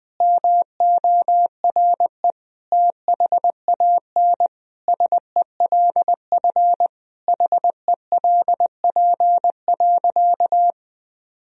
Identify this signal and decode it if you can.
Morse code, heard as a keyed tone, about 21 wpm, 700 Hz. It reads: MORE THAN SELF HELP.
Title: Sentences from Top 1000 Words - Review Part 1 - 20wpm